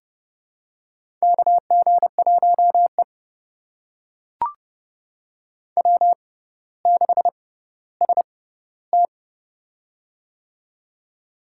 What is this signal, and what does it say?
Morse code, heard as a keyed tone, about 32 wpm, spaced as for 16 wpm, 700 Hz. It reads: KG1E W6ST
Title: USA Call Signs - 6x Character Spacing - 30wpm